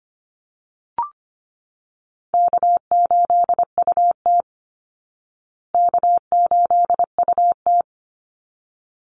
Morse code, heard as a keyed tone, about 25 wpm, 700 Hz. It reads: K8UT K8UT
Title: Easy Call Signs Repeated 2x - 25wpm